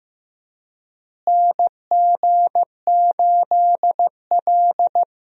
Morse code, heard as a keyed tone, about 16 wpm, 700 Hz. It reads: NG8L
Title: USA Call Signs - 4 Characters - Repeated 3x - 15wpm